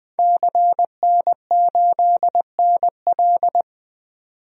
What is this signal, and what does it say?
Morse code, heard as a keyed tone, about 20 wpm, 700 Hz. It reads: CN8NL
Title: DXCC-100 Call Signs - 5 Characters - 20wpm